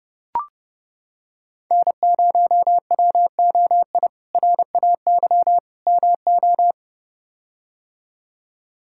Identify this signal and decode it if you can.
Morse code, heard as a keyed tone, about 30 wpm, 700 Hz. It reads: N0WOI RAY MO
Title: K1USN SST QSO Practice - 30wpm